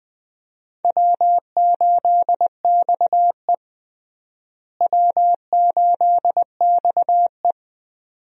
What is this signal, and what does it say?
Morse code, heard as a keyed tone, about 21 wpm, 700 Hz. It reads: W8XE W8XE
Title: USA Call Signs - 4 Characters - Repeated 2x - 20wpm